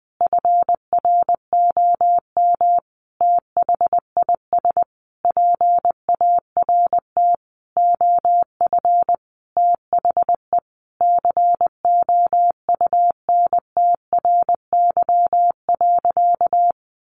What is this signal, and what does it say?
Morse code, heard as a keyed tone, about 20 wpm, 700 Hz. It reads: FROM THIS PART OF THE COUNTRY.
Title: Sentences from Top 300 Words 20wpm